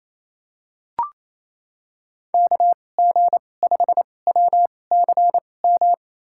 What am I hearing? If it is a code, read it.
Morse code, heard as a keyed tone, about 28 wpm, spaced as for 20 wpm, 700 Hz. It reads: KG5WCM
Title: USA Call Signs - 6 Characters - 2x Character Spacing - 28wpm